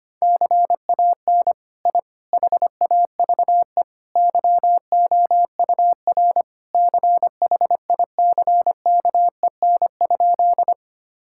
Morse code, heard as a keyed tone, about 25 wpm, 700 Hz. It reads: CAN I HAVE YOUR CHICKEN?
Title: Sentences from Top 1000 Words - Review Part 5 - 25wpm